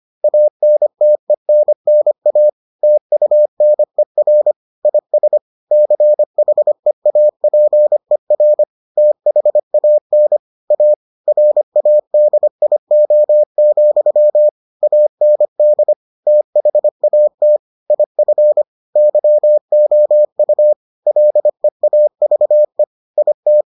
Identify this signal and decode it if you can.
Morse code, heard as a keyed tone, about 25 wpm, 600 Hz. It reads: ANTENNA TUNER IS CHEAPER THAN A RADIO, AND THAT IF YOU LEAVE IT